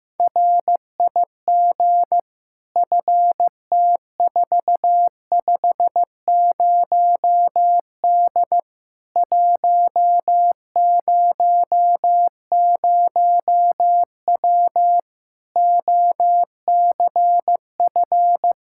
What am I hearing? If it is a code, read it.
Morse code, heard as a keyed tone, about 15 wpm, 700 Hz. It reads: RIG FT450D 100W OCF
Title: Practice QSO - Exchange 2 - Info - 15wpm